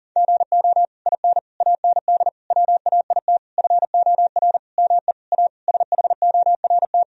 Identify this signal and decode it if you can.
Morse code, heard as a keyed tone, about 40 wpm, 700 Hz. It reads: GO IN AND WAIT FOR ME A SHORT
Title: Sentences from Top 1000 Words - Review Part 2 - 40wpm